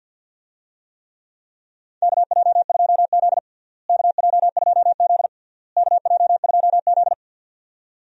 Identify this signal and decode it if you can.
Morse code, heard as a keyed tone, about 52 wpm, 700 Hz. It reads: KJ2Z KJ2Z KJ2Z
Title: USA Call Signs - 4 Characters - Repeated 3x - 50wpm